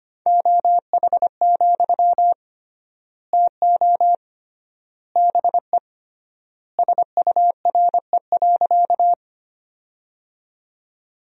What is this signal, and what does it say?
Morse code, heard as a keyed tone, about 25 wpm, 700 Hz. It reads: OH, TO BE SURE.
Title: 3x Word-Spacing - Sentences from Top 500 Words - 25wpm